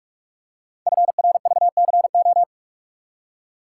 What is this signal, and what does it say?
Morse code, heard as a keyed tone, about 44 wpm, 700 Hz. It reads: FRVCO
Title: Code Groups - 5 Letters - 45wpm